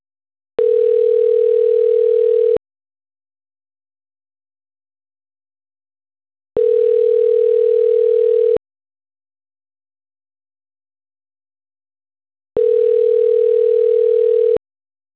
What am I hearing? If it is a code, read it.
Ringback tone (North America), 440 + 480 Hz.
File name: autotestagent / 11-12-53.136..callcentertransfer. .f63efe9b-d445-4fe0-a0ab